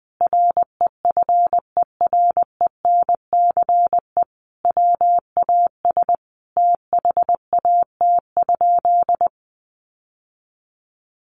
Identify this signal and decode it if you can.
Morse code, heard as a keyed tone, about 20 wpm, 700 Hz. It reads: REFERENCE WAS THAT?
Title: Sentences from Top 1200 Words 20wpm